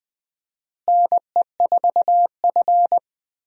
Morse code, heard as a keyed tone, about 20 wpm, 700 Hz. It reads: NE4F
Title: USA Call Signs - 4 Characters - Repeated 2x - 20wpm